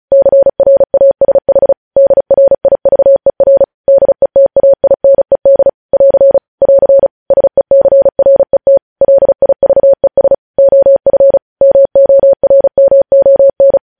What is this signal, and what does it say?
Morse code, heard as a keyed tone, about 36 wpm, 550 Hz. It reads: CRASH DRIVER DETAINED <AR> <AR> SECRET LIVES OF MORMON
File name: podcast / News headlines-1x-35x35